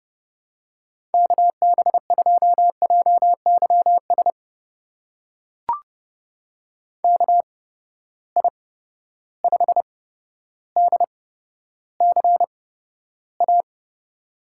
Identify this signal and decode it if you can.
Morse code, heard as a keyed tone, about 31 wpm, spaced as for 13 wpm, 700 Hz. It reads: KB2JYS KI5DCA
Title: USA Call Signs - 6 Characters - 8x Character Spacing - 30wpm